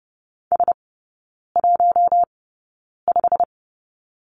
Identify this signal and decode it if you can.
Morse code, heard as a keyed tone, about 30 wpm, spaced as for 9 wpm, 700 Hz. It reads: S15
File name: Lesson 18e - Letter m - Three Characters 30wpm